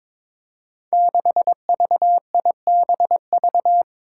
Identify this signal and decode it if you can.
Morse code, heard as a keyed tone, about 22 wpm, 700 Hz. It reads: 6VIBV
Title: Five Challenging Letter-Number Combinations 22wpm